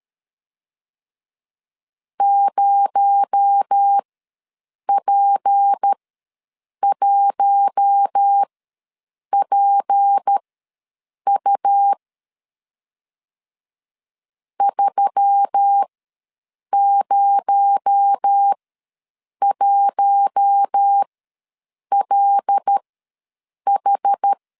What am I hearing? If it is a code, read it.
Morse code, heard as a keyed tone, about 13 wpm, spaced as for 7 wpm, 800 Hz. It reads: 0P1PU 301LH
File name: LEZIONI / 6 PKX0 / 6 PKX0 ws 12wpm 3448